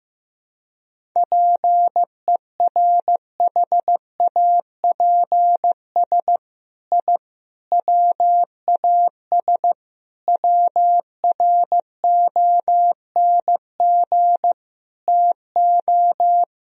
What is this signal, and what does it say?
Morse code, heard as a keyed tone, about 15 wpm, 700 Hz. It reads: PERHAPS I WAS WRONG TO
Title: Sentences from Top 1000 Words - Review Part 1 - 15wpm